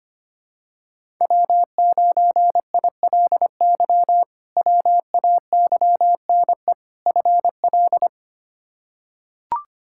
Morse code, heard as a keyed tone, about 25 wpm, 700 Hz. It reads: W9ILY WAYNE FL
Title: K1USN SST QSO Practice - 25wpm